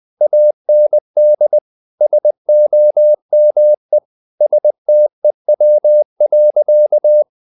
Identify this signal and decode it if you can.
Morse code, heard as a keyed tone, about 20 wpm, 600 Hz. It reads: AND SOME STEW.
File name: the old man and the sea / the old man and the sea part 1 20wpm